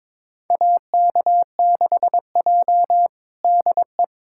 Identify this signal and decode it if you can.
Morse code, heard as a keyed tone, about 21 wpm, 700 Hz. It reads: AK6J DE
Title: USA Callsign DE Callsign - 4 Characters - 22wpm